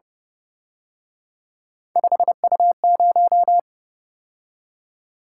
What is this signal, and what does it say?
Morse code, heard as a keyed tone, about 28 wpm, 700 Hz. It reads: 5U0